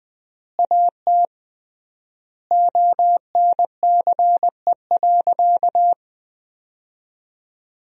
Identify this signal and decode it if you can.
Morse code, heard as a keyed tone, about 20 wpm, 700 Hz. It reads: AT ONCE.
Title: 3x Word-Spacing - Sentences from Top 300 Words - 20wpm